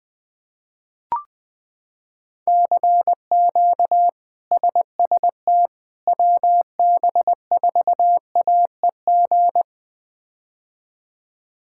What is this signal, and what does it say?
Morse code, heard as a keyed tone, about 20 wpm, 700 Hz. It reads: CQ SST WB4AEG
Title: K1USN SST QSO Practice - 20wpm